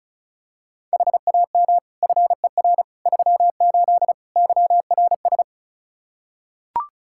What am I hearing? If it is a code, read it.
Morse code, heard as a keyed tone, about 36 wpm, 700 Hz. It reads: HAM FER 38 YRS
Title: K1USN SST QSO Practice 35wpm